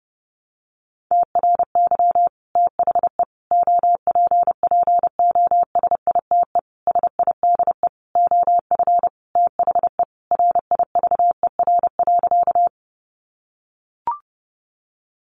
Morse code, heard as a keyed tone, about 28 wpm, 700 Hz. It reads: TRY THE OPPOSITE SIDE OF THE RIVER.
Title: Sentences from Top 1300 Words 30wpm